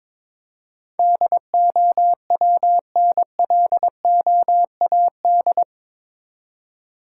Morse code, heard as a keyed tone, about 22 wpm, 700 Hz. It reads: DOWNLOAD
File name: practice / 22wpm / Eight Letter Words - 22wpm